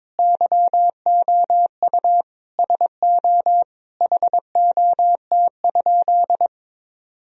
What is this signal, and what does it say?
Morse code, heard as a keyed tone, about 22 wpm, 700 Hz. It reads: YOU SO HOT?
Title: Sentences from Top 1000 Words - Review Part 3 - 22wpm